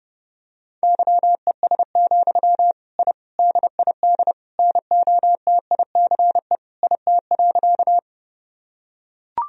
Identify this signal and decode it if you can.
Morse code, heard as a keyed tone, about 30 wpm, 700 Hz. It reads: YES, I DID NOTICE IT.
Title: Sentences from Top 1000 Words - Review Part 2 - 30wpm